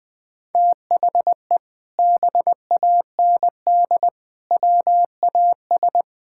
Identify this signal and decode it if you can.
Morse code, heard as a keyed tone, about 20 wpm, 700 Hz. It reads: THE BAND WAS
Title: Sentences with 6-Word Phrases - 20wpm